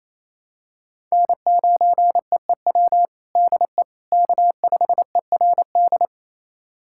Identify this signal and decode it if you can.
Morse code, heard as a keyed tone, about 28 wpm, 700 Hz. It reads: N9EEW DE K5ERD
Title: USA Callsign DE Callsign - 5 Characters - 28wpm